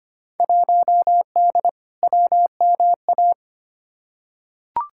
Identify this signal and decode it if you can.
Morse code, heard as a keyed tone, about 25 wpm, 700 Hz. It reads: 1D WMA
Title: Field Day Exchanges - 25wpm